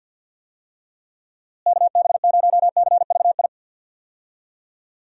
Morse code, heard as a keyed tone, about 51 wpm, 700 Hz. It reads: KB0CVI
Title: USA Call Signs - 6 Characters - Repeated 2x - 50wpm